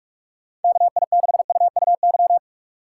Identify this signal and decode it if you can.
Morse code, heard as a keyed tone, about 45 wpm, 700 Hz. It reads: KI6UUY
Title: USA Call Signs - 6 Characters - 2x Character Spacing - 45wpm